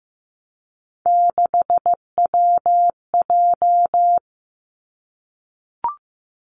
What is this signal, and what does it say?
Morse code, heard as a keyed tone, about 15 wpm, 700 Hz. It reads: 6WJ
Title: Three Challenging Letter-Number Combinations 15wpm